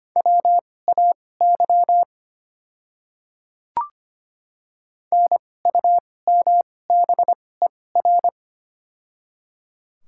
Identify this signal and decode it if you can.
Morse code, heard as a keyed tone, about 25 wpm, spaced as for 18 wpm, 690 Hz. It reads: WAY NUMBER
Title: 2x Character Spacing - Top 200 Words - 25wpm